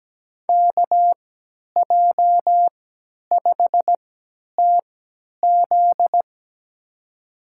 Morse code, heard as a keyed tone, about 17 wpm, spaced as for 10 wpm, 700 Hz. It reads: KJ5TZ
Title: USA Call Signs - 5 Characters - 3x Character Spacing - 17wpm